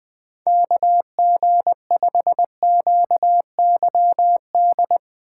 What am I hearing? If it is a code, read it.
Morse code, heard as a keyed tone, about 20 wpm, 700 Hz. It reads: KG5QYD
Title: USA Call Signs - 6 Characters - 4x Character Spacing - 20wpm